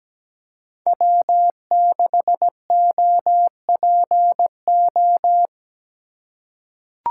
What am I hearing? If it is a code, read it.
Morse code, heard as a keyed tone, about 17 wpm, 700 Hz. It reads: W6OPO
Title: K1USN SST QSO Practice - 17wpm